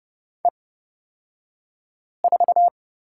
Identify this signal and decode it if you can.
Morse code, heard as a keyed tone, about 30 wpm, spaced as for 5 wpm, 700 Hz. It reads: E4
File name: Lesson 38d - Letter q - Two Characters 30wpm